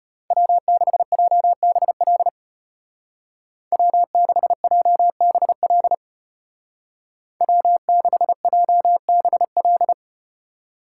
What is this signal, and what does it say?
Morse code, heard as a keyed tone, about 32 wpm, 700 Hz. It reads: W6JBL W6JBL W6JBL